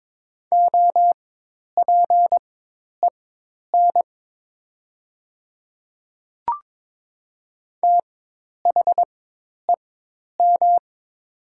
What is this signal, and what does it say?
Morse code, heard as a keyed tone, about 22 wpm, spaced as for 10 wpm, 700 Hz. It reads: OPEN THEM
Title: 4x Character Spacing - Top 300 Words - 22wpm